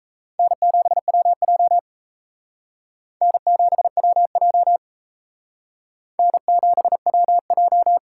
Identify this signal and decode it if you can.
Morse code, heard as a keyed tone, about 35 wpm, 700 Hz. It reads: N7WJ N7WJ N7WJ
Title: Speed Racing - USA Call Signs 28wpm